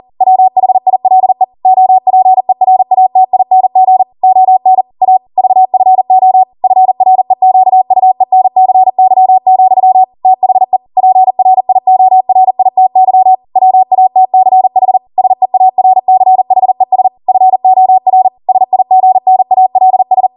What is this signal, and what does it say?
Morse code, heard as a keyed tone, about 40 wpm, 750 Hz. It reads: WHILE OPERATING ON A VFO FREQUENCY, THE PRIORITY WATCH SEARCHES FOR SIGNALS
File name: Morse / 40WPM